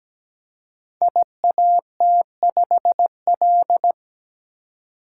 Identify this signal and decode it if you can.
Morse code, heard as a keyed tone, about 17 wpm, 700 Hz. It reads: IAT5L
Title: Code Groups - 5 Letter:Numbers - 17wpm